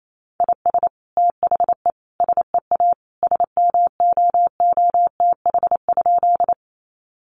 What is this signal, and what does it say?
Morse code, heard as a keyed tone, about 28 wpm, 700 Hz. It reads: IS THE SEA SMOOTH?
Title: Sentences from Top 1400 Words 28wpm